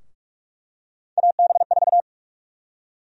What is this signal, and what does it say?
Morse code, heard as a keyed tone, about 47 wpm, 700 Hz. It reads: AB4
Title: USA Call Sign Prefixes - Repeated 2x - 45wpm